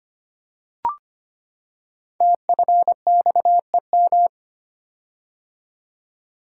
Morse code, heard as a keyed tone, about 25 wpm, 700 Hz. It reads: TFXEM
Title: Code Groups - 5 Letters - 25wpm